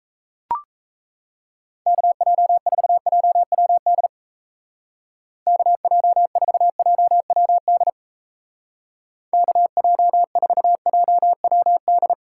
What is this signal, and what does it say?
Morse code, heard as a keyed tone, about 36 wpm, 700 Hz. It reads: KJ4JWD KJ4JWD KJ4JWD